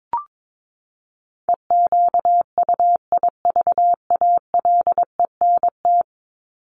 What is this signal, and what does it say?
Morse code, heard as a keyed tone, about 22 wpm, 700 Hz. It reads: EQUIVALENT
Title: Top 1501-1750 Words 22wpm